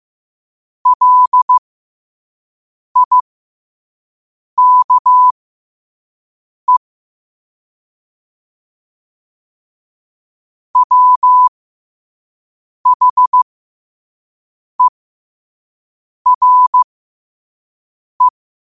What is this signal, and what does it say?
Morse code, heard as a keyed tone, about 15 wpm, spaced as for 5 wpm, 1000 Hz. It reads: LIKE WHERE